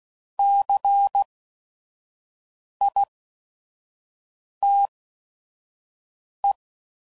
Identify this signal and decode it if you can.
Morse code, heard as a keyed tone, about 16 wpm, spaced as for 5 wpm, 800 Hz. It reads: CITE
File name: Prefixes / By First Letter / Cs